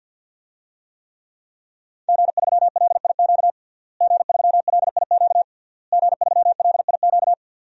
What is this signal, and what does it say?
Morse code, heard as a keyed tone, about 52 wpm, 700 Hz. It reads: G3LIQ G3LIQ G3LIQ
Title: DXCC-100 Call Signs - 5 Characters - Repeated 3x - 50wpm